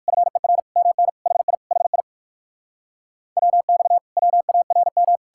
Morse code, heard as a keyed tone, about 69 wpm, spaced as for 58 wpm, 700 Hz. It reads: FER MN HI HI WX WARM